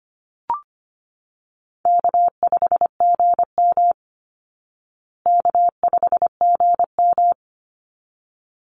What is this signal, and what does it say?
Morse code, heard as a keyed tone, about 25 wpm, 700 Hz. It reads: K5GM K5GM
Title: Easy Call Signs Repeated 2x - 25wpm